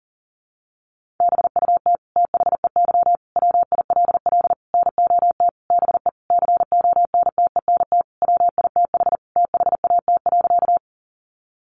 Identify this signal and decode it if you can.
Morse code, heard as a keyed tone, about 40 wpm, 700 Hz. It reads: BUT THEY WILL NOT BE CONTENT WITH THAT.
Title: Sentences from Top 900 Words 40wpm